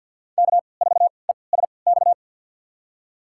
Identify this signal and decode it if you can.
Morse code, heard as a keyed tone, about 50 wpm, spaced as for 28 wpm, 700 Hz. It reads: K4ESX
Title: USA Call Signs - 5 Characters - 3x Character Spacing - 50wpm